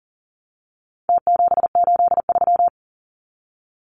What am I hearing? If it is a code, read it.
Morse code, heard as a keyed tone, about 40 wpm, 700 Hz. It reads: T783
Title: Numbers - Contest Style - 4-Digit 40wpm